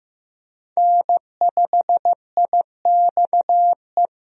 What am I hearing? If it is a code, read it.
Morse code, heard as a keyed tone, about 15 wpm, 700 Hz. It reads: N5IXE
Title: USA Call Signs - 5 Characters - Repeated 3x - 15wpm